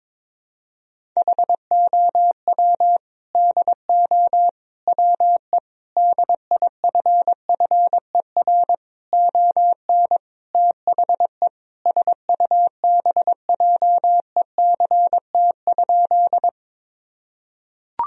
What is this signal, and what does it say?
Morse code, heard as a keyed tone, about 22 wpm, 700 Hz. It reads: HOW DO WE DIFFER ON THE SUBJECT?
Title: Sentences from Top 1000 Words - Review Part 1 - 22wpm